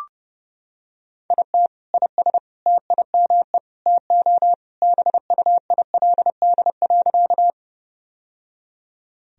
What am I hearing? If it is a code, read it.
Morse code, heard as a keyed tone, about 32 wpm, 700 Hz. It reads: IT IS TIME TO BUILD.
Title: Sentences from Top 1000 Words - Review Part 1 - 30wpm